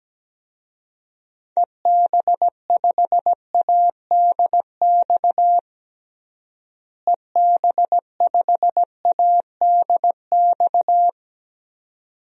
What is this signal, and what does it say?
Morse code, heard as a keyed tone, about 17 wpm, 700 Hz. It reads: EB5ADX EB5ADX
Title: DXCC-100 Call Signs - 6 Characters - Repeated 2x - 17wpm